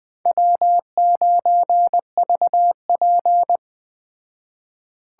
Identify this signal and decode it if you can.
Morse code, heard as a keyed tone, about 20 wpm, 700 Hz. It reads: W9VP